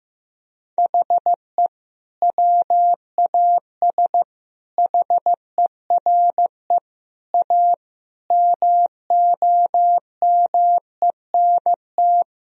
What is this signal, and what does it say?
Morse code, heard as a keyed tone, about 16 wpm, 700 Hz. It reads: HE WAS HERE A MOMENT
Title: Sentences from Top 1000 Words - Review Part 3 - 15wpm